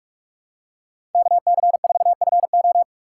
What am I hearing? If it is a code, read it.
Morse code, heard as a keyed tone, about 47 wpm, 700 Hz. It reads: KC4FO
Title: USA Call Signs - 5 Characters - 4x Character Spacing - 45wpm